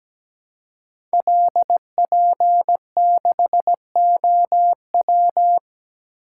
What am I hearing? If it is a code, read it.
Morse code, heard as a keyed tone, about 17 wpm, 700 Hz. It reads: LP6OW
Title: Code Groups - 5 Letter:Numbers - 17wpm